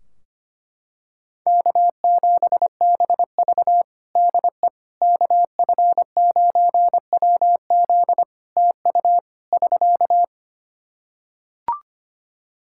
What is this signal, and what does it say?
Morse code, heard as a keyed tone, about 25 wpm, 700 Hz. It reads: K7BV DE KF9WZ TU <SK>
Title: K1USN SST QSO Practice 25wpm